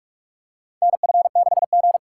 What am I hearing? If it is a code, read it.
Morse code, heard as a keyed tone, about 46 wpm, 690 Hz. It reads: NF6G
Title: USA Call Signs - 4 Characters - 3x Character Spacing - 45wpm